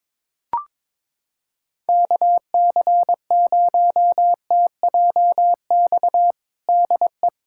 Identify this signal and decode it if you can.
Morse code, heard as a keyed tone, about 22 wpm, 700 Hz. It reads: KC0TJX DE
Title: USA Callsign DE Callsign 22wpm